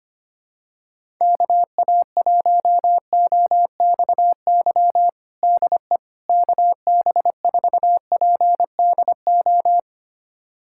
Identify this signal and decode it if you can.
Morse code, heard as a keyed tone, about 25 wpm, 700 Hz. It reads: KA1OXY DE KB4PDO
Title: USA Callsign DE Callsign - 6 Characters - 25wpm